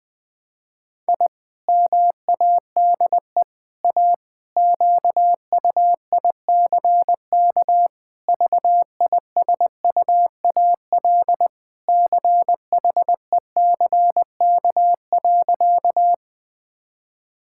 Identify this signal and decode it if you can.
Morse code, heard as a keyed tone, about 21 wpm, 700 Hz. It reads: I MADE A QUICK VISUAL CHECK.
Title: Sentences from Top 1000 Words - Review Part 6 - 20wpm